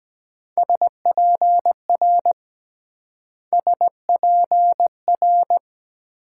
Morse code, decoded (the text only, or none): SPR SPR